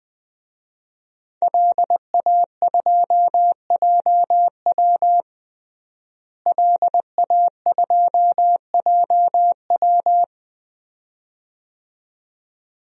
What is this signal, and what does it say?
Morse code, heard as a keyed tone, about 20 wpm, 700 Hz. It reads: LA2JW LA2JW
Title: DXCC-100 Call Signs - 5 Characters - Repeated 2x - 20wpm